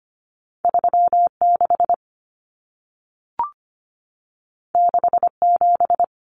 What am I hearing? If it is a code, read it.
Morse code, heard as a keyed tone, about 25 wpm, 700 Hz. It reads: 36 67
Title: Numbers - 2-Digit 25wpm